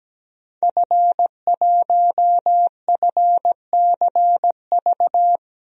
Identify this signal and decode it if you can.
Morse code, heard as a keyed tone, about 17 wpm, 700 Hz. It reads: F1FCV
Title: DXCC-100 Call Signs - 5 Characters - 17wpm